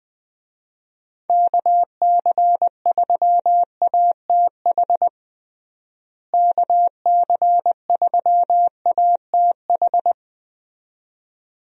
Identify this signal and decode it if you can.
Morse code, heard as a keyed tone, about 20 wpm, 700 Hz. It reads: KC3ATH KC3ATH
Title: USA Call Signs - 6 Characters - Repeated 2x - 20wpm